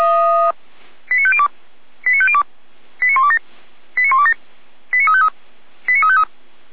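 Five-tone selective call (ZVEI), heard as quick runs of five tones, tones of about 70 ms. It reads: 89502 89502 89127 89127 89353 89353
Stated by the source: ZVEI five-tone selective call received by SDR, from the feuerwehr-orlamuende.org (fire brigade Orlamünde) server.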